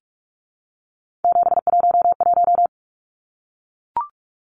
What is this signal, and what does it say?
Morse code, heard as a keyed tone, about 44 wpm, 700 Hz. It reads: 711
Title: Numbers - 3-Digit 45wpm